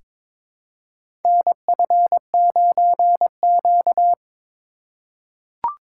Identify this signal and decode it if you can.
Morse code, heard as a keyed tone, about 22 wpm, 700 Hz. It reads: NF9Q
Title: USA Call Signs - 4x Character Spacing - 22wpm